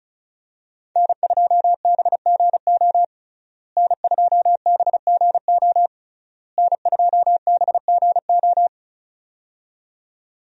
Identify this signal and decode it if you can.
Morse code, heard as a keyed tone, about 34 wpm, 700 Hz. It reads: N2BGO N2BGO N2BGO
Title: USA Call Signs - 5 Characters - Repeated 3x - 35wpm